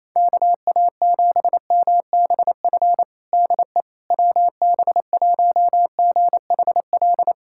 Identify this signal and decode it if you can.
Morse code, heard as a keyed tone, about 28 wpm, 700 Hz. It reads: KA7MBF DE WB1GHL
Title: USA Callsign DE Callsign - 6 Characters - 28wpm